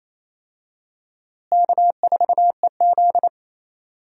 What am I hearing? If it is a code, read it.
Morse code, heard as a keyed tone, about 28 wpm, 700 Hz. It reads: K4EZ
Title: USA Call Signs - 4 Characters - Repeated 3x - 28wpm